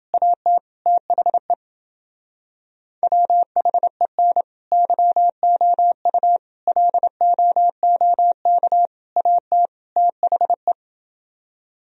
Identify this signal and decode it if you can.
Morse code, heard as a keyed tone, about 29 wpm, 700 Hz. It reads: AT THE WHEN YOU LOOK AT THE